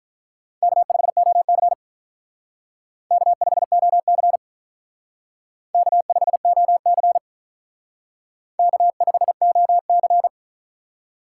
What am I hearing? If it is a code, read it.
Morse code, heard as a keyed tone, about 45 wpm, 700 Hz. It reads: K5OC K5OC K5OC K5OC